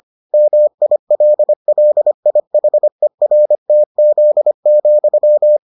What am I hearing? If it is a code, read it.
Morse code, heard as a keyed tone, about 25 wpm, 600 Hz. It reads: MILLIHERTZ,